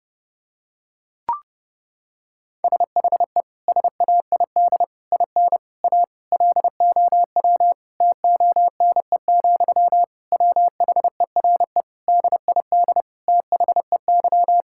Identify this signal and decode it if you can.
Morse code, heard as a keyed tone, about 28 wpm, 700 Hz. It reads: SHE SAID IN A LOW TONE, WHERE DID THEY
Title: Sentences from Top 1600 Words 30wpm